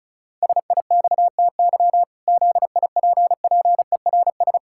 Morse code, heard as a keyed tone, about 34 wpm, 700 Hz. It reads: SIXTY ZIPPERS
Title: Pangram Challenge v2 - 35wpm